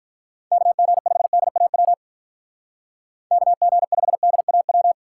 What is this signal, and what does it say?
Morse code, heard as a keyed tone, about 44 wpm, 700 Hz. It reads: KG5DAW KG5DAW